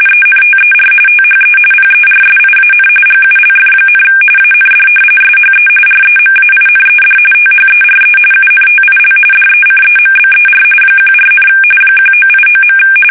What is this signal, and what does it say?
RTTY, about 45.45 baud. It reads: ENTER THE PERSIAN GULF, AND SHE REMAINED THERE UNTIL DECEMBER 14, 1989.
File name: fsk850 / battleshipnj